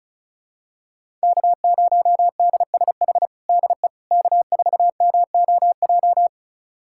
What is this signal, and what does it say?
Morse code, heard as a keyed tone, about 34 wpm, 700 Hz. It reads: K0DSH DE K4MOJ
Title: USA Callsign DE Callsign - 5 Characters - 35wpm